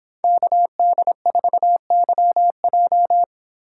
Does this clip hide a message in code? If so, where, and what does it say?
Yes: Morse code from 0.2 to 3.2 s, heard as a keyed tone, about 25 wpm, 700 Hz. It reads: KD4YJ